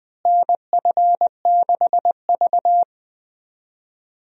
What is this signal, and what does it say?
Morse code, heard as a keyed tone, about 20 wpm, 700 Hz. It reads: NF6V
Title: USA Call Signs - 4 Characters - Repeated 3x - 20wpm